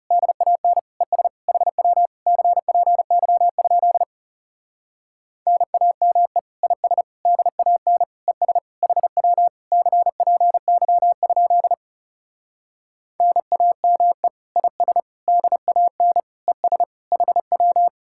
Morse code, heard as a keyed tone, about 39 wpm, 700 Hz. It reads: DAN ES HW CPY? NAME IS DAN ES HW CPY? NAME IS DAN ES HW